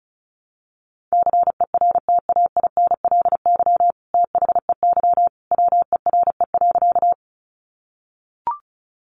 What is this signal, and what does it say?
Morse code, heard as a keyed tone, about 34 wpm, 700 Hz. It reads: CERTAINLY THEY WERE.
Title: Sentences from Top 900 Words 35wpm